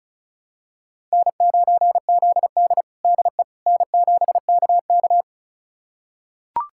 Morse code, heard as a keyed tone, about 36 wpm, 700 Hz. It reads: N9ZD DE N7KK
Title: USA Callsign DE Callsign - 4 Characters - 35wpm